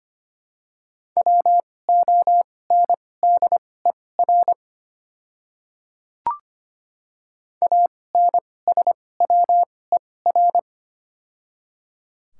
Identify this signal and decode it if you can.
Morse code, heard as a keyed tone, about 25 wpm, spaced as for 18 wpm, 700 Hz. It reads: WONDER ANSWER
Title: 2x Character Spacing - Top 500 Words - 25wpm